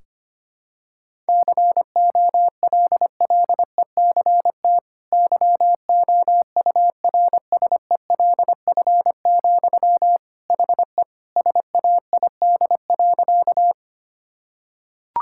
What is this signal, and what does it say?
Morse code, heard as a keyed tone, about 25 wpm, 700 Hz. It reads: COLLECT YOURSELF, HE SAID.
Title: Sentences from Top 1600 Words 25wpm